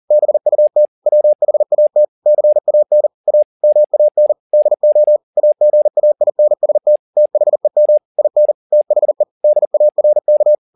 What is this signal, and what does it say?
Morse code, heard as a keyed tone, about 40 wpm, 600 Hz. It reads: BUT WHAT CAN A MAN DO AGAINST THEM IN THE DARK
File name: the old man and the sea 2 40wpm